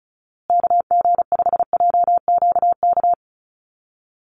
Morse code, heard as a keyed tone, about 34 wpm, 700 Hz. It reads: KG5JQK
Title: USA Call Signs Repeated 2x 35wpm